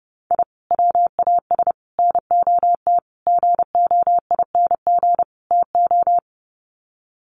Morse code, heard as a keyed tone, about 30 wpm, 700 Hz. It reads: I WAS NOT GOING TO
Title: Sets of 5 Words 30wpm